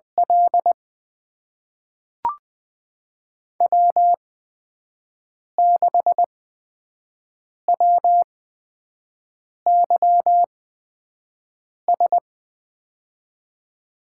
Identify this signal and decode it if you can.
Morse code, heard as a keyed tone, about 20 wpm, spaced as for 5 wpm, 700 Hz. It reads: L W6WYS
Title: USA Call Signs - 8x Character Spacing - 20wpm